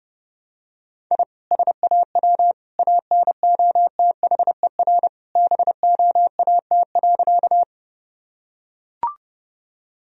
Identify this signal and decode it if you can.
Morse code, heard as a keyed tone, about 30 wpm, 700 Hz. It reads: I SAW ANOTHER BOAT.
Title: Sentences from Top 1000 Words - Review Part 3 - 30wpm